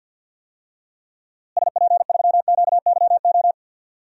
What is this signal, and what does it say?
Morse code, heard as a keyed tone, about 53 wpm, 700 Hz. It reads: SP3QYO
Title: DXCC-100 Call Signs - 6 Characters - Repeated 3x - 50wpm